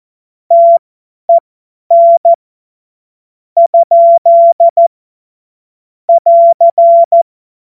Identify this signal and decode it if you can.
Morse code, heard as a keyed tone, about 14 wpm, spaced as for 8 wpm, 680 Hz. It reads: TEN? <AR>
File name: Morsekurs / Witze / Witz 25. April 2023 50 F70